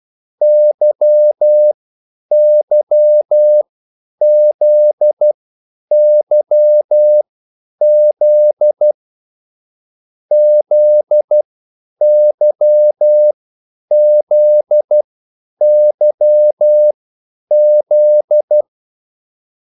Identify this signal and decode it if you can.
Morse code, heard as a keyed tone, about 12 wpm, spaced as for 9 wpm, 600 Hz. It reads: YYZYZ ZYZYZ